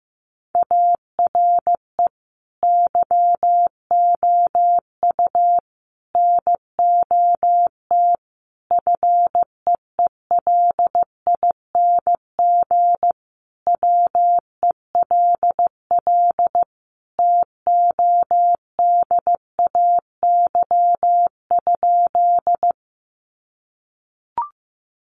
Morse code, heard as a keyed tone, about 15 wpm, 700 Hz. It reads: ARE YOU NOT FEELING WELL TODAY?
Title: Sentences from Top 1200 Words 15wpm